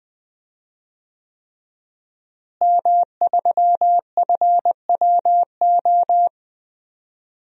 Morse code, heard as a keyed tone, about 20 wpm, 700 Hz. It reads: M3FWO
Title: DXCC-100 Call Signs - 5 Characters - Repeated 3x - 20wpm